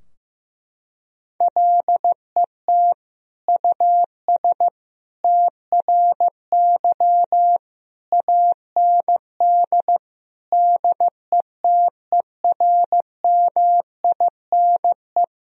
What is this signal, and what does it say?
Morse code, heard as a keyed tone, about 15 wpm, 700 Hz. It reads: LET US TRY AND DETERMINE
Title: Sentences from Top 1000 Words - Review Part 4 - 15wpm